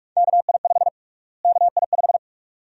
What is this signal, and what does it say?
Morse code, heard as a keyed tone, about 46 wpm, 700 Hz. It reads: KI5 KI5
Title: USA Call Sign Prefixes - Repeated 2x - 45wpm